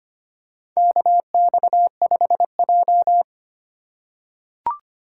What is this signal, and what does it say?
Morse code, heard as a keyed tone, about 25 wpm, 700 Hz. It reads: KX5J
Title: USA Call Signs - 4 Characters - 3x Character Spacing - 25wpm